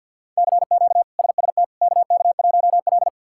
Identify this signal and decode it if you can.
Morse code, heard as a keyed tone, about 51 wpm, 700 Hz. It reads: CQ SST KK1L
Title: K1USN SST QSO Practice - 50wpm